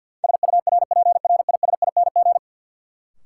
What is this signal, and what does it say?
Morse code, heard as a keyed tone, about 52 wpm, 700 Hz. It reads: SURPRISING
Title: Sets of 5 Words - New Word Drill - 50wpm